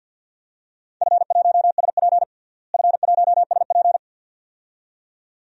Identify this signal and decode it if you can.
Morse code, heard as a keyed tone, about 48 wpm, 700 Hz. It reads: F1SP F1SP
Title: DXCC-100 Call Signs - 4 Characters - Repeated 2x - 50wpm